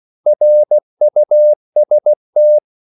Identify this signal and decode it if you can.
Morse code, heard as a keyed tone, about 16 wpm, 600 Hz. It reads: RUST
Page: Korte danske ord 16wpm